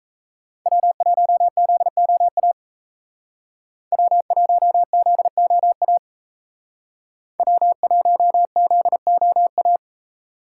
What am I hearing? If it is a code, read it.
Morse code, heard as a keyed tone, about 36 wpm, 700 Hz. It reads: W1ZOA W1ZOA W1ZOA